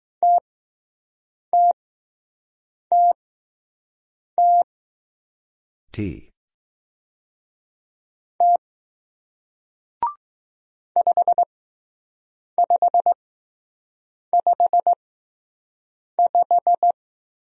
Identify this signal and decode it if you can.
Morse code, heard as a keyed tone, about 20 wpm, spaced as for 6 wpm, 700 Hz. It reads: TTTT T 5555